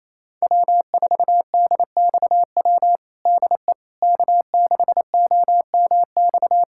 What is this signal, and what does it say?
Morse code, heard as a keyed tone, about 28 wpm, 700 Hz. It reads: W4DXW DE K6OMX
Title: USA Callsign DE Callsign - 5 Characters - 28wpm